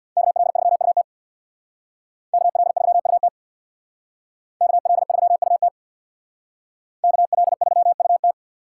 Morse code, heard as a keyed tone, about 64 wpm, 700 Hz. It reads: KL3UT KL3UT KL3UT KL3UT